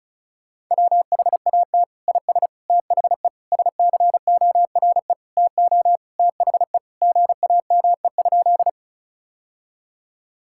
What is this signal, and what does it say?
Morse code, heard as a keyed tone, about 36 wpm, 700 Hz. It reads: WHAT IS THE SCORE TO THE GAME?
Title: Sentences from Top 1600 Words 35wpm